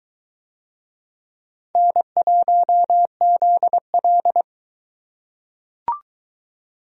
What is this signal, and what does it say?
Morse code, heard as a keyed tone, about 22 wpm, 700 Hz. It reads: N1ZL